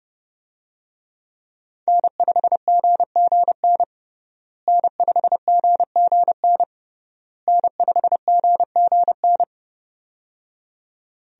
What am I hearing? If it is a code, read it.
Morse code, heard as a keyed tone, about 30 wpm, 700 Hz. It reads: N5GGN N5GGN N5GGN
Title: USA Call Signs - 5 Characters - Repeated 3x - 30wpm